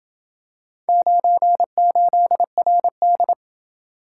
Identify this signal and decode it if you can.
Morse code, heard as a keyed tone, about 27 wpm, 700 Hz. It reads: 98RD